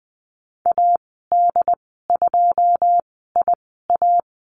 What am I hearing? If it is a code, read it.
Morse code, heard as a keyed tone, about 20 wpm, spaced as for 15 wpm, 700 Hz. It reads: AD2IA
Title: 082 Lesson 14g - Number 2 - Call Signs 20wpm